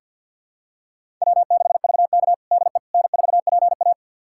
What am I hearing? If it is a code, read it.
Morse code, heard as a keyed tone, about 50 wpm, 700 Hz. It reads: W6VK DE N4PA
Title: USA Callsign DE Callsign - 4 Characters - 50wpm